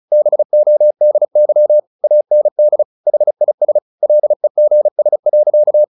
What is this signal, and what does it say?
Morse code, heard as a keyed tone, about 34 wpm, 600 Hz. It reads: BODY AND HIS LEGS.
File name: the old man and the sea 2 35wpm